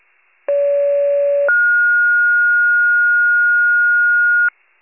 Two-tone page, heard as a one-second tone, then a three-second tone, about 570 then 1430 Hz. It is heard from 0.5 to 4.5 s.
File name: wcalert